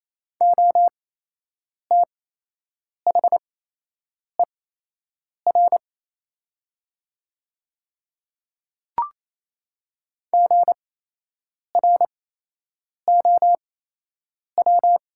Morse code, heard as a keyed tone, about 27 wpm, spaced as for 8 wpm, 700 Hz. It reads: OTHER GROW
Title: 8x Character Spacing - Top 300 Words - 28wpm